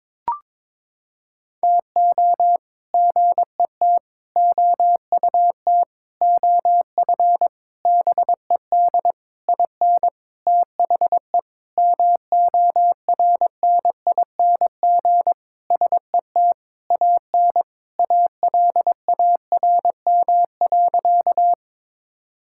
Morse code, heard as a keyed tone, about 22 wpm, 700 Hz. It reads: TO GET OUT OF BED IN THE MORNING SET AN ALARM.
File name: practice / 22wpm / Sentences with 8-Word Phrases - 22wpm